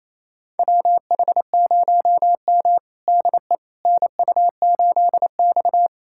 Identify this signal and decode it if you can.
Morse code, heard as a keyed tone, about 27 wpm, 700 Hz. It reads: WH0M DE NU8X
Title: USA Callsign DE Callsign - 4 Characters - 28wpm